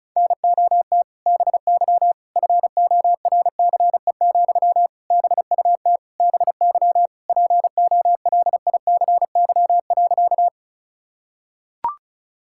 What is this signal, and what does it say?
Morse code, heard as a keyed tone, about 34 wpm, 700 Hz. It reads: NOT BY FORCE, BUT BY POLICY.
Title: Sentences from Top 1000 Words - Review Part 5 - 35wpm